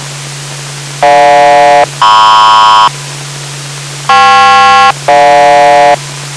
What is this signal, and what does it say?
Aircraft SELCAL, heard as two pairs of tones. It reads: GJ-MP KP-GJ